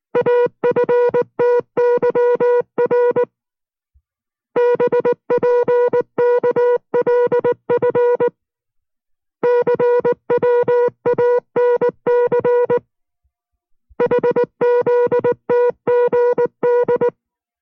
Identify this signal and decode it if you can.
Morse code, heard as a keyed tone, about 19 wpm, 480 Hz. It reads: AFTYR BPKLF CWANC HZTGD